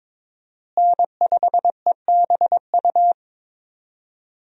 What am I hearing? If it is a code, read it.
Morse code, heard as a keyed tone, about 22 wpm, 700 Hz. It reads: N5EBU
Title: USA Call Signs - 5 Characters - Repeated 3x - 22wpm